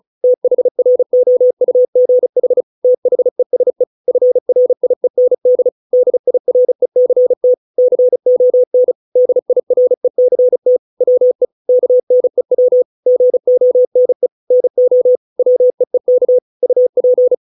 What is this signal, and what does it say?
Morse code, heard as a keyed tone, about 36 wpm, 500 Hz. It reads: THROUGH THESE FRIEND DIRECT CON DIRECT WE KNEW GONE NO WEEK UP